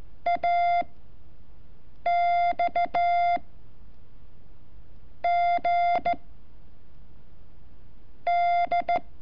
Morse code, heard as a keyed tone, about 13 wpm, 700 Hz. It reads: A X G D
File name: curso / ejercicio 30